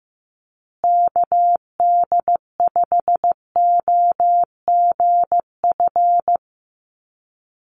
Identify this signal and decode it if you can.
Morse code, heard as a keyed tone, about 15 wpm, 700 Hz. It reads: KD5OGF
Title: USA Call Signs Repeated 2x 15wpm